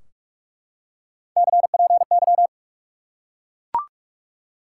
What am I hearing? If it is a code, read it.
Morse code, heard as a keyed tone, about 44 wpm, 700 Hz. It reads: CPY
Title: Top 180 QSO Elements 45wpm